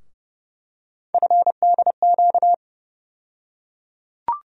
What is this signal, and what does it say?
Morse code, heard as a keyed tone, about 30 wpm, 700 Hz. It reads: FDQ
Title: USA Call Sign Postfixes - 3 Letters 30wpm